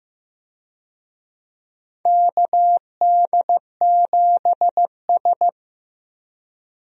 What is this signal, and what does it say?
Morse code, heard as a keyed tone, about 15 wpm, 700 Hz. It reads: KD7S
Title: USA Call Signs - 4 Characters - Repeated 3x - 15wpm